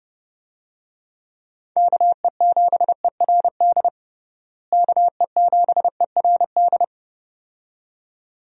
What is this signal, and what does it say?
Morse code, heard as a keyed tone, about 30 wpm, 700 Hz. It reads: KE7ERD KE7ERD
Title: USA Call Signs - 6 Characters - Repeated 2x - 30wpm